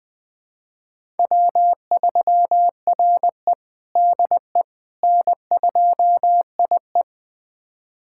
Morse code, heard as a keyed tone, about 20 wpm, 700 Hz. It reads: W3RE DE N2IE
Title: USA Callsign DE Callsign - 4 Characters - 20wpm